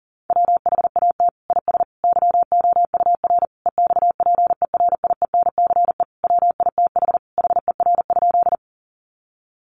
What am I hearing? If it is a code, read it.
Morse code, heard as a keyed tone, about 40 wpm, 700 Hz. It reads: WHAT IS YOUR EXPERIENCE WITH HER?
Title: Sentences from Top 600 Words 40wpm